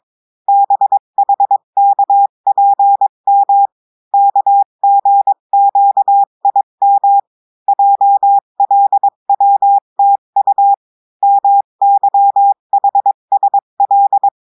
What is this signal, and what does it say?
Morse code, heard as a keyed tone, about 22 wpm, 800 Hz. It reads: BHKPM KGQIM JLWTU MYHSL
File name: cours-du-mercredi / audios / cours-853-du-2-mai-2024-etxte chiffre et texte clair-a-20-wpm